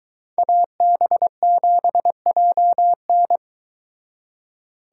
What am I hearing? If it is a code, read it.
Morse code, heard as a keyed tone, about 23 wpm, 700 Hz. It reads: AB7JN